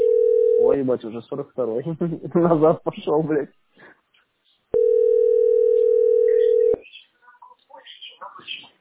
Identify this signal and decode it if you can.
Ringback tone (North America), 440 + 480 Hz, with speech.
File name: 0f452bb9-d639-436e-b4fe